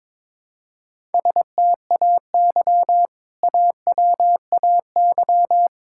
Morse code, heard as a keyed tone, about 22 wpm, 700 Hz. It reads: STAY AWAY
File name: practice / 22wpm / Sentences with 2-Word Phrases - 22wpm